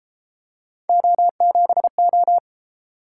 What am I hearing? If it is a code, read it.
Morse code, heard as a keyed tone, about 32 wpm, 700 Hz. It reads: O7O